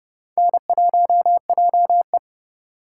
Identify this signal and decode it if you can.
Morse code, heard as a keyed tone, about 28 wpm, 700 Hz. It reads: N1JE